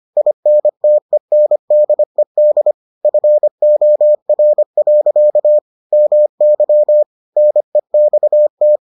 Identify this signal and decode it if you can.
Morse code, heard as a keyed tone, about 25 wpm, 600 Hz. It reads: INTENDED FOR. MY NEXT